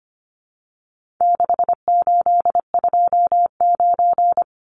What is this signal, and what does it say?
Morse code, heard as a keyed tone, about 25 wpm, 700 Hz. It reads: 6829
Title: Numbers - 4-Digit 25wpm